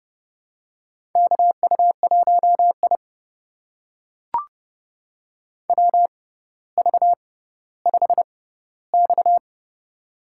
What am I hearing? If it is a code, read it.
Morse code, heard as a keyed tone, about 30 wpm, spaced as for 15 wpm, 700 Hz. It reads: KU1I WV5X
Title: USA Call Signs - 6x Character Spacing - 30wpm